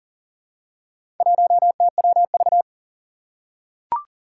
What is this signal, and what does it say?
Morse code, heard as a keyed tone, about 40 wpm, 700 Hz. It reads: 1TWV
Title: Four Challenging Letter-Number Combinations 40wpm